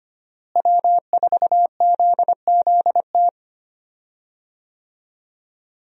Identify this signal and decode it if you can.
Morse code, heard as a keyed tone, about 25 wpm, 700 Hz. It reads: W4ZZT
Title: USA Call Signs - 5 Characters - Repeated 3x - 25wpm